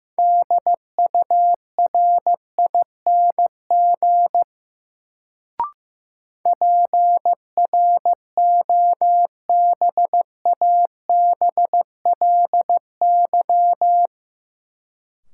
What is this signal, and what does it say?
Morse code, heard as a keyed tone, about 15 wpm, 700 Hz. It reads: DURING PROBABLY
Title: Sentences from Top 700 Words - New Word Drill - 15wpm